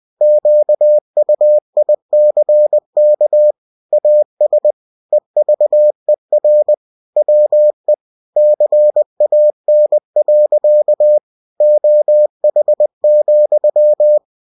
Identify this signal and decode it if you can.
Morse code, heard as a keyed tone, about 20 wpm, 600 Hz. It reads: QUICK AS EVER WE CAN. OH,
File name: cw-books / winnie-the-pooh / winnie-20wpm-page-0098